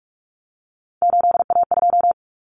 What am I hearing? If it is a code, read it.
Morse code, heard as a keyed tone, about 44 wpm, 700 Hz. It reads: 8A2
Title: Common Cut Numbers - 3-Digit 45wpm